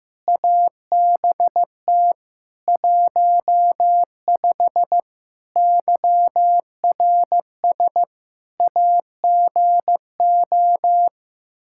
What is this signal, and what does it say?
Morse code, heard as a keyed tone, about 15 wpm, 700 Hz. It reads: ABT 15 YRS AGO
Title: Practice QSO - Exchange 3 - More Info - 15wpm